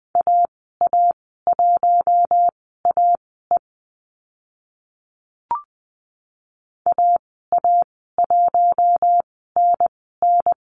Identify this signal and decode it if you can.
Morse code, heard as a keyed tone, about 20 wpm, spaced as for 14 wpm, 700 Hz. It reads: AA1AE AA1NN
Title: 042 Lesson 8g - Number 1 - Call Signs 20wpm